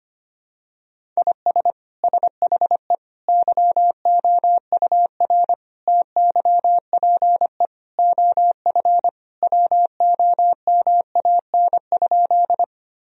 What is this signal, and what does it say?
Morse code, heard as a keyed tone, about 25 wpm, 700 Hz. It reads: IS SHE YOUR TYPE OF WOMAN?
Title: Sentences from Top 1000 Words - Review Part 3 - 25wpm